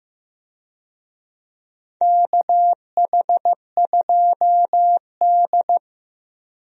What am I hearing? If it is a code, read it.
Morse code, heard as a keyed tone, about 15 wpm, 700 Hz. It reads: KH2D
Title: USA Call Signs - 4 Characters - Repeated 2x - 15wpm